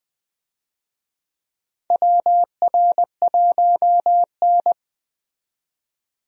Morse code, heard as a keyed tone, about 21 wpm, 700 Hz. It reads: WR1N
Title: USA Call Signs - 4 Characters - 20wpm